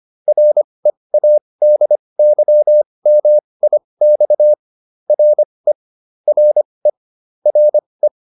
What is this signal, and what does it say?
Morse code, heard as a keyed tone, about 25 wpm, spaced as for 20 wpm, 600 Hz. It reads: READYMIX RE RE RE